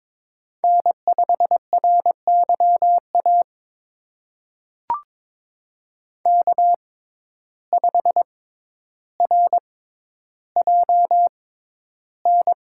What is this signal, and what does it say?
Morse code, heard as a keyed tone, about 22 wpm, spaced as for 11 wpm, 700 Hz. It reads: N5RYA K5RJN
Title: USA Call Signs - 5 Characters - 6x Character Spacing - 22wpm